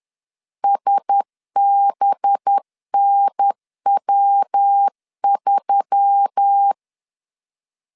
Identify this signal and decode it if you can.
Morse code, heard as a keyed tone, about 10 wpm, 800 Hz. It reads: SBNW3